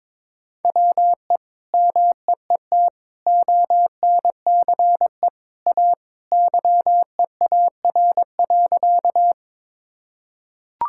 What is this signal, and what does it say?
Morse code, heard as a keyed tone, about 22 wpm, 700 Hz. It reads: WE MEET ONCE A YEAR.
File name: practice / 22wpm / Sentences with 2-Word Phrases - 22wpm